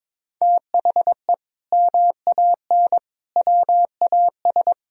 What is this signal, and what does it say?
Morse code, heard as a keyed tone, about 21 wpm, 700 Hz. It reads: THE MAN WAS
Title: Sentences from Top 1000 Words - Review Part 4 - 22wpm